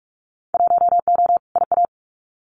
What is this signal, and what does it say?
Morse code, heard as a keyed tone, about 44 wpm, 700 Hz. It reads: 1O IA
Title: Winter Field Day Exchanges 45wpm